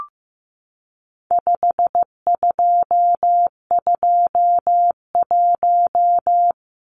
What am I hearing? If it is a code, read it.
Morse code, heard as a keyed tone, about 15 wpm, 700 Hz. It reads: 5221
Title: Numbers - 4-Digit 15wpm